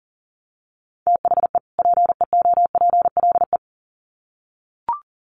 Morse code, heard as a keyed tone, about 40 wpm, 700 Hz. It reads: THE PEOPLE
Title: Sets of 2 Words 40wpm